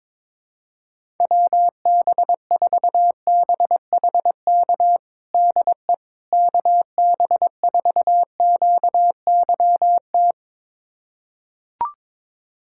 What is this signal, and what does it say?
Morse code, heard as a keyed tone, about 22 wpm, 700 Hz. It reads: WB4BHK DE KB4QYT
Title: USA Callsign DE Callsign - 6 Characters - 22wpm